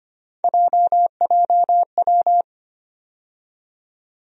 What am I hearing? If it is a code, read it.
Morse code, heard as a keyed tone, about 25 wpm, 700 Hz. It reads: JJW
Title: USA Call Sign Postfixes - 3 Letters 25wpm